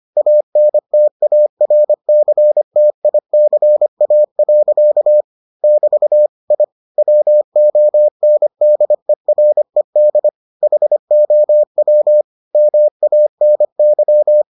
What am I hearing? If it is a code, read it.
Morse code, heard as a keyed tone, about 25 wpm, 600 Hz. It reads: ANTARCTICA. <BT> I WONDERED HOW MANY